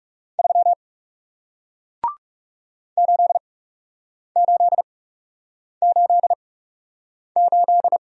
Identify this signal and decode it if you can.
Morse code, heard as a keyed tone, about 40 wpm, spaced as for 8 wpm, 700 Hz. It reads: 3 8888